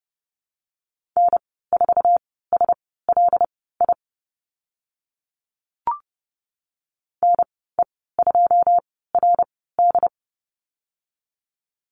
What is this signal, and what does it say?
Morse code, heard as a keyed tone, about 30 wpm, spaced as for 17 wpm, 700 Hz. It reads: N4SLI NE2RD
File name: Lesson 14g - Number 2 - Call Signs 30wpm